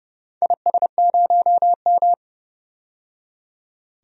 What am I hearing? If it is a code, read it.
Morse code, heard as a keyed tone, about 32 wpm, 700 Hz. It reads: IS0M
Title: DXCC-100 Call Signs - 4 Characters - 30wpm